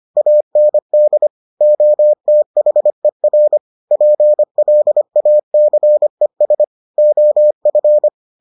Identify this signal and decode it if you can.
Morse code, heard as a keyed tone, about 25 wpm, 600 Hz. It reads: AND OTHER PLACES OF